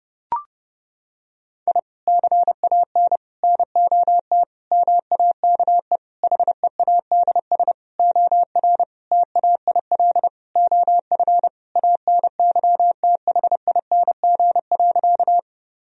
Morse code, heard as a keyed tone, about 28 wpm, 700 Hz. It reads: I CAN NOT MAKE HEADS OR TAIL OF ANYTHING.
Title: Sentences from Top 1000 Words - Review Part 3 - 30wpm